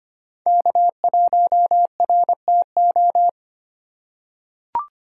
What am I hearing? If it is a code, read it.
Morse code, heard as a keyed tone, about 25 wpm, 700 Hz. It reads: K1RTO
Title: USA Call Signs - 5 Characters - 2x Character Spacing - 25wpm